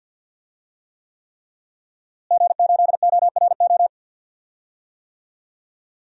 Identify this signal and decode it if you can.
Morse code, heard as a keyed tone, about 46 wpm, 700 Hz. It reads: G8ORO
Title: DXCC-100 Call Signs - 5 Characters - Repeated 3x - 50wpm